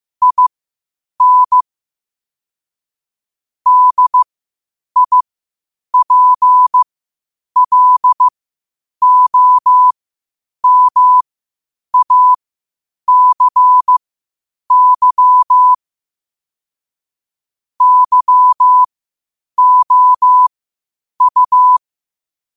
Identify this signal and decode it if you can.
Morse code, heard as a keyed tone, about 15 wpm, spaced as for 8 wpm, 1000 Hz. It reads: IN DIPLOMACY YOU
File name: qotdfiles / 8wpm